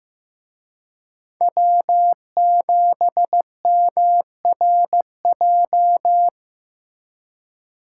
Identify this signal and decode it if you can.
Morse code, heard as a keyed tone, about 19 wpm, 700 Hz. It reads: W7MRJ
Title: Speed Racing - USA Call Signs 15wpm